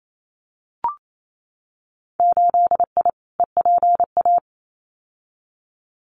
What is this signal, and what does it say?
Morse code, heard as a keyed tone, about 28 wpm, 700 Hz. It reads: 8I EPA
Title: Winter Field Day Exchanges Encore 28wpm